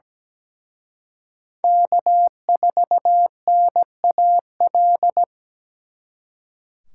Morse code, heard as a keyed tone, about 20 wpm, 700 Hz. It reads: K4NAL
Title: Speed Racing - USA 1x3 Call Signs 17wpm